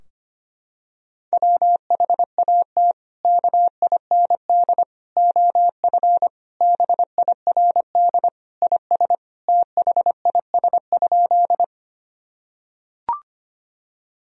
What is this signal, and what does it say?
Morse code, heard as a keyed tone, about 25 wpm, 700 Hz. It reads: WHAT KIND OF BIRD IS THIS?
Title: Sentences from Top 1000 Words - Review Part 2 - 25wpm